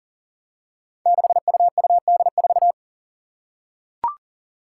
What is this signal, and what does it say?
Morse code, heard as a keyed tone, about 40 wpm, 700 Hz. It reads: 6UUD4
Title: Five Challenging Letter-Number Combinations 40wpm